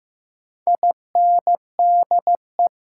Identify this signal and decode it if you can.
Morse code, heard as a keyed tone, about 16 wpm, 700 Hz. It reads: INDE
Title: Sentences from Top 900 Words - New Word Drill - 15wpm